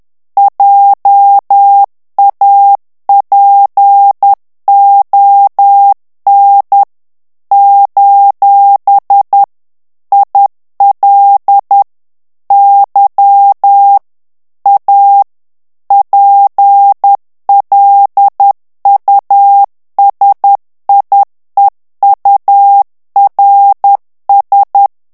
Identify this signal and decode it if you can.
Morse code, heard as a keyed tone, about 10 wpm, 800 Hz. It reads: JAPON : IL Y A PLUSIEURS
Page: Français - texte en clair 1 - vitesse : 10 mots minute